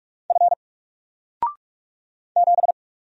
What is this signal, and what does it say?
Morse code, heard as a keyed tone, about 46 wpm, 700 Hz. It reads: F 7
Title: Single Letter-Number 45wpm